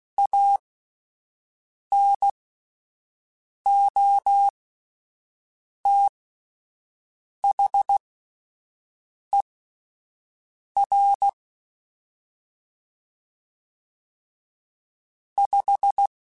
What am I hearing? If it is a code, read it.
Morse code, heard as a keyed tone, about 16 wpm, spaced as for 5 wpm, 800 Hz. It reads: ANOTHER 5
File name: (spaced) / QSOs Spaced-mp3s / WA7QOX 1 @10 sp